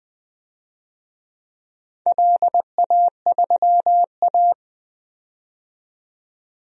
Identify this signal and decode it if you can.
Morse code, heard as a keyed tone, about 20 wpm, 700 Hz. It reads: LA3A
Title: DXCC-100 Call Signs - 4 Characters - 20wpm